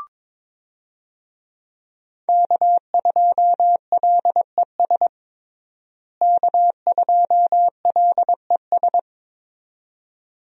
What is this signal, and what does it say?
Morse code, heard as a keyed tone, about 22 wpm, 700 Hz. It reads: K2LES K2LES
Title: USA Call Signs - 5 Characters - Repeated 2x - 22wpm